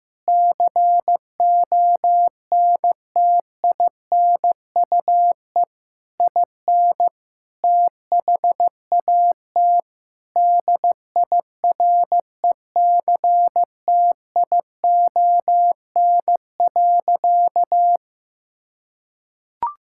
Morse code, heard as a keyed tone, about 15 wpm, 700 Hz. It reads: CONTINUE IN THAT DIRECTION.
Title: Sentences from Top 1000 Words - Review Part 6 - 15wpm